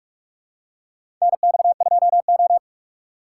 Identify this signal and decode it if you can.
Morse code, heard as a keyed tone, about 46 wpm, 700 Hz. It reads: NX2O
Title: USA Call Signs - 4 Characters - 2x Character Spacing - 45wpm